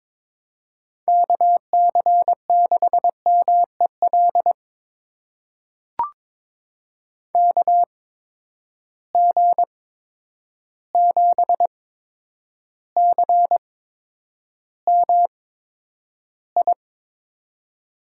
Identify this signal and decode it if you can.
Morse code, heard as a keyed tone, about 22 wpm, spaced as for 9 wpm, 700 Hz. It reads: KC6MEL KG7CMI
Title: USA Call Signs - 6 Characters - 8x Character Spacing - 22wpm